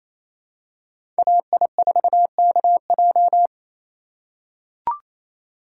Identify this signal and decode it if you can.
Morse code, heard as a keyed tone, about 28 wpm, 700 Hz. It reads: AI4KJ
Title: USA Call Signs - 2x Character Spacing - 28wpm